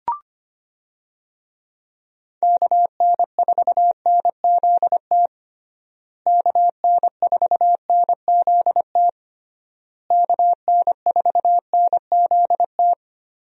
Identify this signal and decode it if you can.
Morse code, heard as a keyed tone, about 25 wpm, 700 Hz. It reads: KN4NZT KN4NZT KN4NZT
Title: USA Call Signs - 6 Characters - Repeated 3x - 25wpm